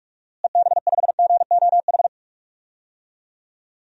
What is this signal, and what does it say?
Morse code, heard as a keyed tone, about 44 wpm, 700 Hz. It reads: EB5GOH
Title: DXCC-100 Call Signs - 6 Characters - Repeated 2x - 45wpm